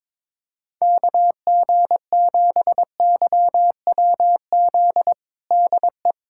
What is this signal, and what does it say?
Morse code, heard as a keyed tone, about 22 wpm, 700 Hz. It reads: KG7YWZ DE
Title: USA Callsign DE Callsign - 6 Characters - 22wpm